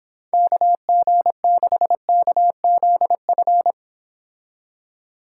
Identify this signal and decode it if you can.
Morse code, heard as a keyed tone, about 26 wpm, 700 Hz. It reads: KG6KZF